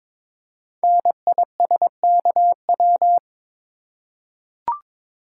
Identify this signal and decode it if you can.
Morse code, heard as a keyed tone, about 22 wpm, 700 Hz. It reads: NISKW
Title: Code Groups - 5 Letters - 22wpm